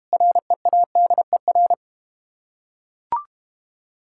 Morse code, heard as a keyed tone, about 31 wpm, 690 Hz. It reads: READER
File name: practice / 25wpm / Difficult Words - Speed-Racing with Hints - 25wpm